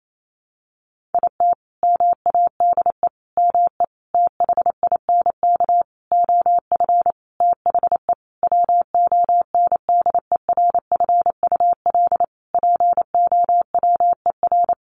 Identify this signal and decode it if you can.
Morse code, heard as a keyed tone, about 28 wpm, 700 Hz. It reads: IT MADE ME THINK OF THE WONDERFUL POWER
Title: Sentences from Top 1100 Words 28wpm